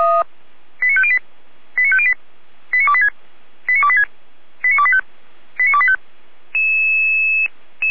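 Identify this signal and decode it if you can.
Five-tone selective call (ZVEI), heard as quick runs of five tones, tones of about 70 ms. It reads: 89508 89508 89286 89286 89285 89285
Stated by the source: ZVEI five-tone selective call received by SDR.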